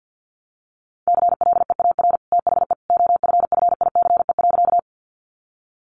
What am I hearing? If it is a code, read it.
Morse code, heard as a keyed tone, about 50 wpm, 700 Hz. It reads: CLEAR THE OFFICE.
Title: Sentences from Top 1000 Words 50wpm